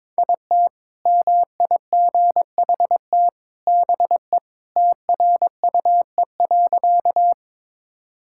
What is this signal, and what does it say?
Morse code, heard as a keyed tone, about 22 wpm, 700 Hz. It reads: IT MIGHT BE TRUE.
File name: practice / 22wpm / Sentences with 2-Word Phrases - 22wpm